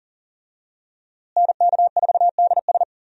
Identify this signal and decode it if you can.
Morse code, heard as a keyed tone, about 38 wpm, 700 Hz. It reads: NK4DS
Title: USA Call Signs - 5 Characters - Repeated 2x - 40wpm